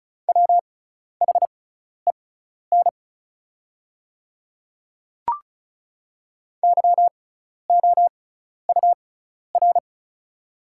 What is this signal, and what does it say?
Morse code, heard as a keyed tone, about 34 wpm, spaced as for 12 wpm, 700 Hz. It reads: WHEN YOUR
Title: 6x Character Spacing - Top 100 Words - 35wpm